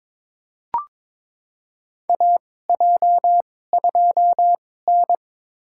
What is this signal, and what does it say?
Morse code, heard as a keyed tone, about 22 wpm, spaced as for 16 wpm, 700 Hz. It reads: AJ2N
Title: USA Call Signs - 2x Character Spacing - 22wpm